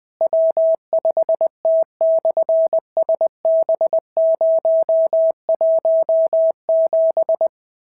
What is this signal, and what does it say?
Morse code, heard as a keyed tone, about 20 wpm, 650 Hz. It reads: W5T/SB017
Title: SOTA QSO Practice 20wpm